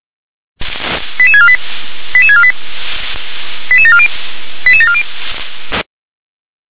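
Five-tone selective call (ZVEI), heard as quick runs of five tones, tones of about 70 ms. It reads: 88648 88648 88740 88740
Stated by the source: Five-tone ZVEI selective call received on SDR.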